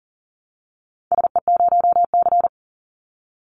Morse code, heard as a keyed tone, about 41 wpm, 700 Hz. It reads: SE0C
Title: Call Signs - 40wpm